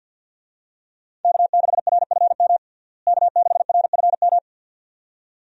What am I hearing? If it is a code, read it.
Morse code, heard as a keyed tone, about 52 wpm, 700 Hz. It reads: K6RFM K6RFM
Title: USA Call Signs - 5 Characters - Repeated 2x - 50wpm